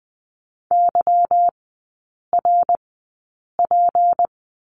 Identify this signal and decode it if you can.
Morse code, heard as a keyed tone, about 20 wpm, spaced as for 8 wpm, 700 Hz. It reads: YRP
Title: 172 Lesson 28e - Number 9 - Three Characters 20wpm